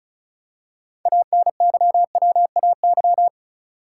Morse code, heard as a keyed tone, about 36 wpm, 700 Hz. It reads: ANYWAY
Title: Six Letter Words - 35wpm